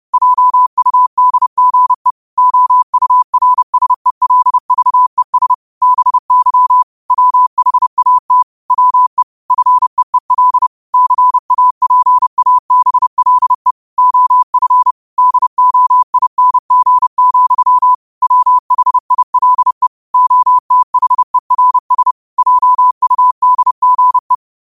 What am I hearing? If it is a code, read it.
Morse code, heard as a keyed tone, about 30 wpm, 1000 Hz. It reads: JUDGE OURSELVES BY WHAT WE FEEL CAPABLE OF DOING, WHILE OTHERS JUDGE
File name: qotdfiles / 30wpm